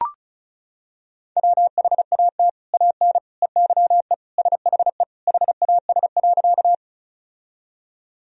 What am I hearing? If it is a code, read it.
Morse code, heard as a keyed tone, about 35 wpm, 700 Hz. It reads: WHAT AN EYE SHE HAS.
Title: Sentences from Top 1000 Words - Review Part 1 - 35wpm